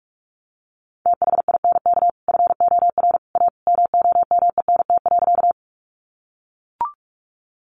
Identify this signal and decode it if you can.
Morse code, heard as a keyed tone, about 44 wpm, 700 Hz. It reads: THINK FOR A MOMENT.
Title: Sentences from Top 1000 Words 45wpm